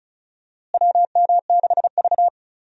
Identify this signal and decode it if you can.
Morse code, heard as a keyed tone, about 36 wpm, 700 Hz. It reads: WM6V
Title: USA Call Signs - 4 Characters - Repeated 2x - 35wpm